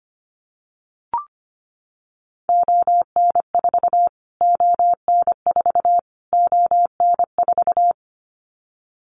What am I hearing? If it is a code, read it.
Morse code, heard as a keyed tone, about 25 wpm, 700 Hz. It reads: ON4 ON4 ON4
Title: Callsign Prefixes for Top 25 DXCC - Repeated 3x - 25wpm